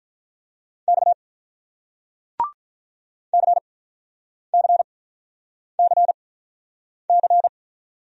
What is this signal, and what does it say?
Morse code, heard as a keyed tone, about 48 wpm, spaced as for 9 wpm, 700 Hz. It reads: X CCCC